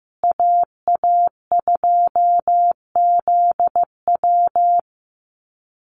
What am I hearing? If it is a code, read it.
Morse code, heard as a keyed tone, about 15 wpm, 700 Hz. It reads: AA2ZW
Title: Call Signs Repeated 2x - 15wpm